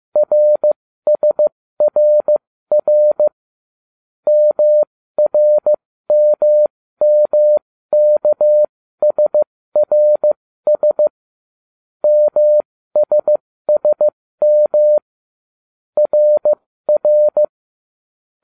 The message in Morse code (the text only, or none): RSRR MRMMKSRS MSSM RR